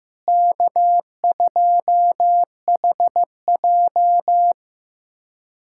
Morse code, heard as a keyed tone, about 15 wpm, 700 Hz. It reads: K2HJ
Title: USA Callsign DE Callsign - 4 Characters - 15wpm